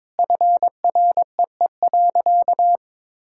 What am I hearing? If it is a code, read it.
Morse code, heard as a keyed tone, about 22 wpm, 700 Hz. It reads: FREE.
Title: Sentences from Top 1000 Words - Review Part 5 - 22wpm